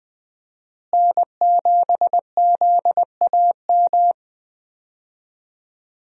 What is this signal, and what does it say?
Morse code, heard as a keyed tone, about 20 wpm, 700 Hz. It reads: N7ZAM
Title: USA Call Signs - 5 Characters - Repeated 3x - 20wpm